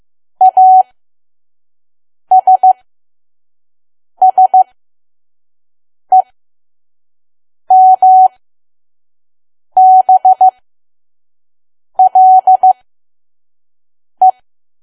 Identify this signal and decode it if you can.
Morse code, heard as a keyed tone, about 15 wpm, spaced as for 5 wpm, 750 Hz. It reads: ASSEMBLE
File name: Archive / 5 WPM files / PM